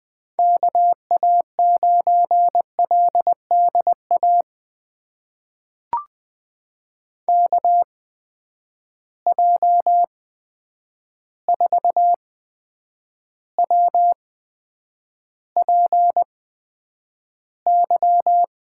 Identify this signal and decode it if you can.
Morse code, heard as a keyed tone, about 20 wpm, spaced as for 9 wpm, 700 Hz. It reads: KA9LDA KJ4WPY
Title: USA Call Signs - 6 Characters - 8x Character Spacing - 20wpm